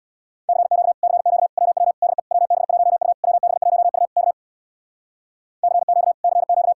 Morse code, heard as a keyed tone, about 61 wpm, 700 Hz. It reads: CQ CQ WW DE WB1U WB1U K CQ CQ
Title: Speed-Racing Common QSO Phrases Encore - 50wpm